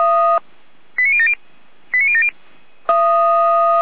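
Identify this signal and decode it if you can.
Five-tone selective call (ZVEI), heard as quick runs of five tones, tones of about 70 ms. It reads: 89070 89070
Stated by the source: ZVEI five-tone selective call received on SDR, archived by Feuerwehr Orlamünde.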